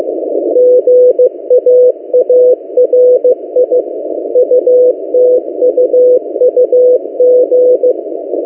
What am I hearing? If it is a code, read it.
Morse code, heard as a keyed tone, about 15 wpm, 500 Hz. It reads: GAARI UTUUG
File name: M08a- -1355utc-8096khz